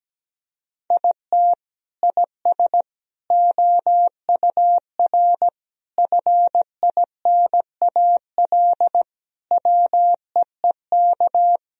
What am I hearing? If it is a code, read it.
Morse code, heard as a keyed tone, about 17 wpm, 700 Hz. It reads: IT IS OUR FINAL WEEK
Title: Sentences from Top 1000 Words - Review Part 2 - 17wpm